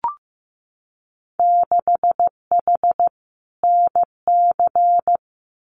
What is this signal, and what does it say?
Morse code, heard as a keyed tone, about 15 wpm, 700 Hz. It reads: 6H NC
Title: Winter Field Day Exchanges Encore 15wpm